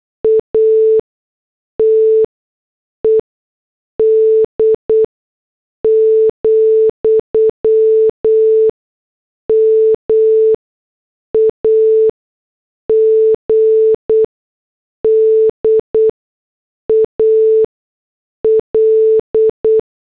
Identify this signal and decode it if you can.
Morse code, heard as a keyed tone, about 8 wpm, spaced as for 6 wpm, 440 Hz. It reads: ATED, MAGDAL